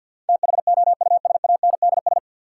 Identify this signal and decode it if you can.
Morse code, heard as a keyed tone, about 51 wpm, 700 Hz. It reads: THOUSANDS
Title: Sets of 4 Words - New Word Drill - 50wpm